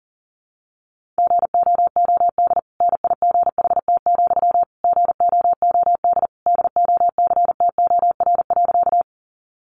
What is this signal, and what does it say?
Morse code, heard as a keyed tone, about 40 wpm, 700 Hz. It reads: GOOD NIGHT, GOOD DOCTOR.
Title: Sentences from Top 1500 Words 40wpm